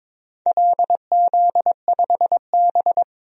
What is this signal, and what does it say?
Morse code, heard as a keyed tone, about 22 wpm, 700 Hz. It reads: LZ5B
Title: DXCC-100 Call Signs - 4 Characters - 22wpm